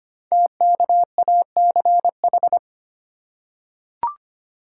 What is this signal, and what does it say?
Morse code, heard as a keyed tone, about 25 wpm, 700 Hz. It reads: TKACH
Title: Code Groups - 5 Letters - 25wpm